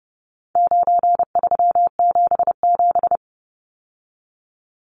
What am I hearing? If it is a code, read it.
Morse code, heard as a keyed tone, about 30 wpm, 700 Hz. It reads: 9377
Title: Numbers - 4-Digit 30wpm